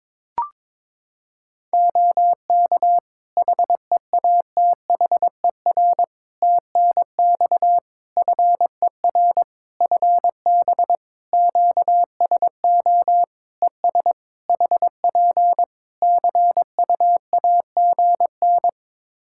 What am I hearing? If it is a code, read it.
Morse code, heard as a keyed tone, about 22 wpm, 700 Hz. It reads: OK HEATHER TNX FER FB QSO ES HP CUAGN
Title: K1USN SST QSO Practice 22wpm